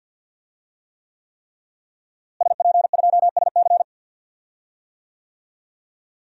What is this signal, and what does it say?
Morse code, heard as a keyed tone, about 48 wpm, 700 Hz. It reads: SP2SC
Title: DXCC-100 Call Signs - 5 Characters - Repeated 3x - 50wpm